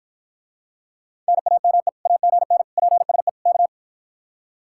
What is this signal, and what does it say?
Morse code, heard as a keyed tone, about 52 wpm, 700 Hz. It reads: NAME AGN PSE K